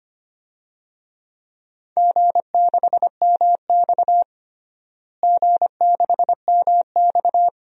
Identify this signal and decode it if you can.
Morse code, heard as a keyed tone, about 25 wpm, 700 Hz. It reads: G6MX G6MX
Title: DXCC-100 Call Signs - 4 Characters - Repeated 3x - 25wpm